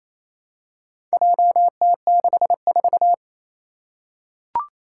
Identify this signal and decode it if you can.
Morse code, heard as a keyed tone, about 28 wpm, 700 Hz. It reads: JT64
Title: Four Challenging Letter-Number Combinations 28wpm